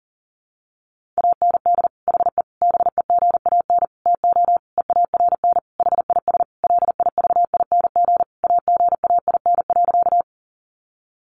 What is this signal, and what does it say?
Morse code, heard as a keyed tone, about 40 wpm, 700 Hz. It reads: AND HE BEGAN TO EARN HIS LIVING AGAIN.
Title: Sentences from Top 1500 Words 40wpm